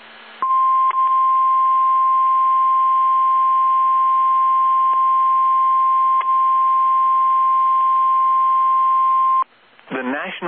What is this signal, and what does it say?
Emergency Alert System (SAME) tones (the 1050 Hz attention tone). It is heard from 0.4 to 9.5 s.
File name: eas / KSDZKDJL / KSDZKDJL